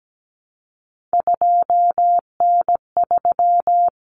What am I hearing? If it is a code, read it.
Morse code, heard as a keyed tone, about 17 wpm, 700 Hz. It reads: 2N3
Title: Common Cut Numbers - 3-Digit 17wpm